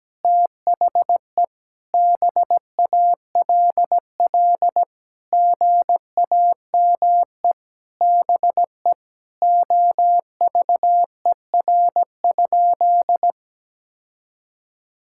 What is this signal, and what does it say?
Morse code, heard as a keyed tone, about 17 wpm, 700 Hz. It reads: THE BALL GAME BE OVER?
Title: Sentences from Top 1000 Words - Review Part 6 - 17wpm